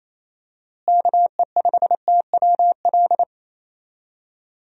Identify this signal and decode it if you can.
Morse code, heard as a keyed tone, about 28 wpm, 700 Hz. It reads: KE5TWL
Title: USA Call Signs - 6 Characters - Repeated 3x - 28wpm